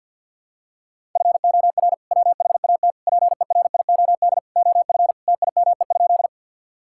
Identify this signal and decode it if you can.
Morse code, heard as a keyed tone, about 51 wpm, 700 Hz. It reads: FOR WHAT PERIOD OF TIME?
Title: Sentences from Top 1000 Words - Review Part 5 - 50wpm